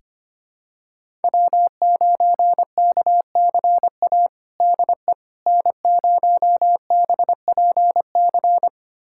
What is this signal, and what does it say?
Morse code, heard as a keyed tone, about 25 wpm, 700 Hz. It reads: W9KCA DE N0BPC
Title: USA Callsign DE Callsign - 5 Characters - 25wpm